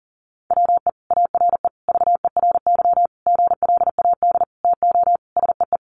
Morse code, heard as a keyed tone, about 40 wpm, 700 Hz. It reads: WE ARE VERY GLAD TO SEE
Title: Sentences from Top 800 Words 40wpm